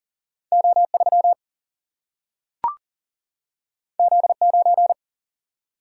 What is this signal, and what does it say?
Morse code, heard as a keyed tone, about 40 wpm, 700 Hz. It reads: O 3 Z 9
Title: Single-Letter Single-Number 40wpm